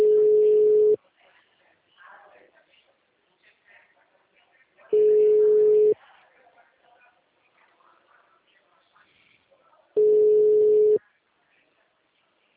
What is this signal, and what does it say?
Ringback tone (Europe, 425 Hz), with speech.